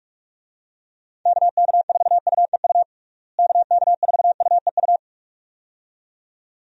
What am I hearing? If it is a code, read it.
Morse code, heard as a keyed tone, about 46 wpm, 700 Hz. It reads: KK4UEU KK4UEU
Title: USA Call Signs - 6 Characters - Repeated 2x - 45wpm